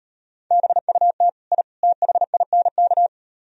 Morse code, heard as a keyed tone, about 38 wpm, 700 Hz. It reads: BUT I THINK